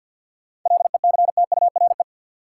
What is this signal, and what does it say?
Morse code, heard as a keyed tone, about 48 wpm, 700 Hz. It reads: LECTURE
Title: Sentences from Top 1600 Words - New Word Drill - 50wpm